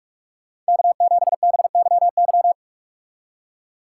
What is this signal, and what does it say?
Morse code, heard as a keyed tone, about 59 wpm, 700 Hz. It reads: K7BYY